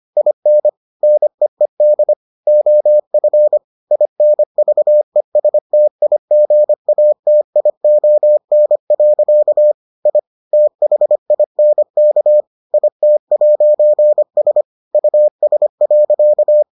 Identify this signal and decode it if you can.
Morse code, heard as a keyed tone, about 25 wpm, 600 Hz. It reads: IN NEED OF INVESTIGATION. I THINK IT'S US.